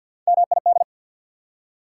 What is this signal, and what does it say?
Morse code, heard as a keyed tone, about 53 wpm, 690 Hz. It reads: MID
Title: Prefixes and Suffixes 50wpm